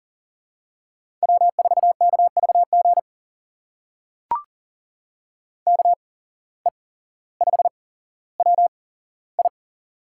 Morse code, heard as a keyed tone, about 39 wpm, spaced as for 17 wpm, 700 Hz. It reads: W4KVG KE5WI
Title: USA Call Signs - 5 Characters - 8x Character Spacing - 40wpm